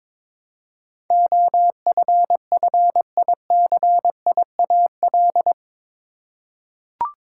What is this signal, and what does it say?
Morse code, heard as a keyed tone, about 21 wpm, 700 Hz. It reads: OFFICIAL
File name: practice / 22wpm / Eight Letter Words - 22wpm